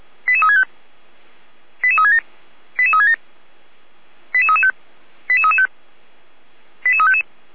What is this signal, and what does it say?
Five-tone selective call (ZVEI), heard as quick runs of five tones, tones of about 70 ms. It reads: 88365 88367 88367 88335 88335 88355